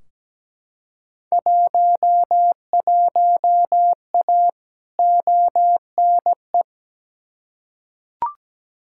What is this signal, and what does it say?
Morse code, heard as a keyed tone, about 17 wpm, 700 Hz. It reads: 11A ONE
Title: Field Day Exchanges Encore - 17wpm